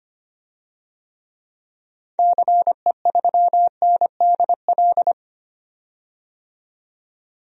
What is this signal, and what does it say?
Morse code, heard as a keyed tone, about 25 wpm, 700 Hz. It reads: CE3NDL
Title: DXCC-100 Call Signs - 6 Characters - 25wpm